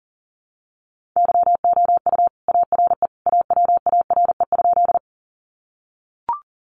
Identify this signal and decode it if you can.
Morse code, heard as a keyed tone, about 40 wpm, 700 Hz. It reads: YOU ARE AWARE?
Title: Sentences from Top 1200 Words 40wpm